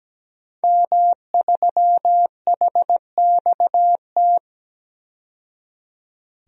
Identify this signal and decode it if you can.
Morse code, heard as a keyed tone, about 17 wpm, 700 Hz. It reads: M3HXT
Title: DXCC-100 Call Signs - 5 Characters - 17wpm